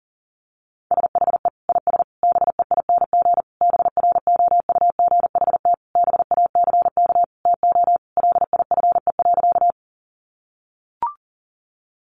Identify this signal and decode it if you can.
Morse code, heard as a keyed tone, about 40 wpm, 700 Hz. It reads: SHE IS BEING BROUGHT BACK TO LIFE.
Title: Sentences from Top 600 Words 40wpm